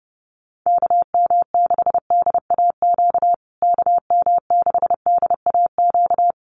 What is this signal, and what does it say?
Morse code, heard as a keyed tone, about 30 wpm, 700 Hz. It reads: KM6DAQ KM6DAQ
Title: USA Call Signs Repeated 3x 30wpm